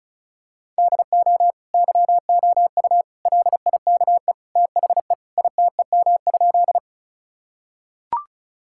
Morse code, heard as a keyed tone, about 36 wpm, 700 Hz. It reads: DO YOU LIKE THE ITEM?
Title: Sentences from Top 1000 Words - Review Part 6 - 35wpm